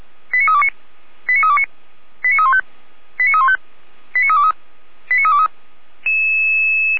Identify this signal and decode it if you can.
Five-tone selective call (ZVEI), heard as quick runs of five tones, tones of about 70 ms. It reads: 89329 89329 89315 89315 89323 89323